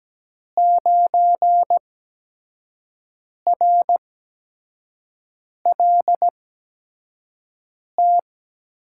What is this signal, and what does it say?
Morse code, heard as a keyed tone, about 17 wpm, spaced as for 5 wpm, 700 Hz. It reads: 9RLT
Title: USA Call Signs - 8x Character Spacing - 17wpm